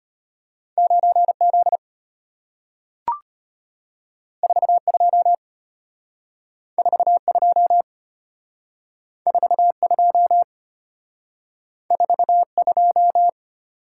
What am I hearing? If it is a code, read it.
Morse code, heard as a keyed tone, about 33 wpm, 690 Hz. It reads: 9Z 42 42 42 42